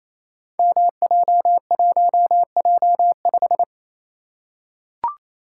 Morse code, heard as a keyed tone, about 27 wpm, 700 Hz. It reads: MJ1J5
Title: Code Groups - 5 Letter:Numbers - 28wpm